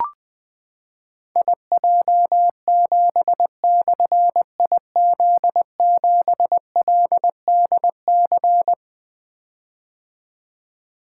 Morse code, heard as a keyed tone, about 21 wpm, 700 Hz. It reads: IJ7/IZ7LDC
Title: Difficult Call Signs 20wpm